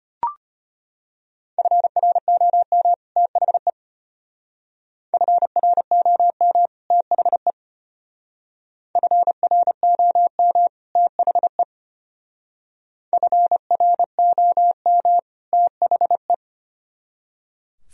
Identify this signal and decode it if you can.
Morse code, heard as a keyed tone, about 32 wpm, 700 Hz. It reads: FROM THE FROM THE FROM THE FROM THE